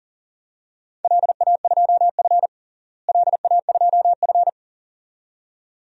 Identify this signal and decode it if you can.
Morse code, heard as a keyed tone, about 37 wpm, 700 Hz. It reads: LA2F LA2F
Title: DXCC-100 Call Signs - 4 Characters - Repeated 2x - 40wpm